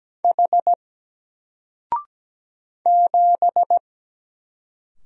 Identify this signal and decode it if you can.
Morse code, heard as a keyed tone, about 17 wpm, 700 Hz. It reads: H 7
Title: Single Letter-Number 17wpm